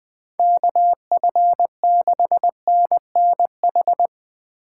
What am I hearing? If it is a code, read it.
Morse code, heard as a keyed tone, about 19 wpm, 700 Hz. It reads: KF6NNH
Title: Speed Racing - USA 2x3 Call Signs 17wpm